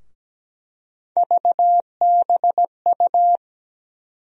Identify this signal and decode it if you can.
Morse code, heard as a keyed tone, about 17 wpm, 700 Hz. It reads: VBU
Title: USA Call Sign Postfixes - 3 Letters 17wpm